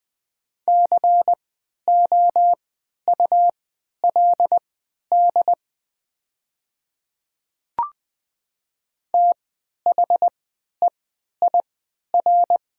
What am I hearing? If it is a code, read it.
Morse code, heard as a keyed tone, about 20 wpm, spaced as for 11 wpm, 700 Hz. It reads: COULD THEIR
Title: 3x Character Spacing - Top 200 Words - 20wpm